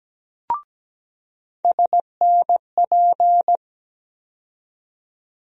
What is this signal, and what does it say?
Morse code, heard as a keyed tone, about 17 wpm, 700 Hz. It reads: SNP
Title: USA Call Sign Postfixes - 3 Letters 17wpm